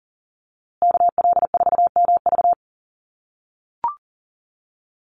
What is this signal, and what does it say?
Morse code, heard as a keyed tone, about 40 wpm, 700 Hz. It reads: KL4MV
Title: USA Call Signs Repeated 2x 40wpm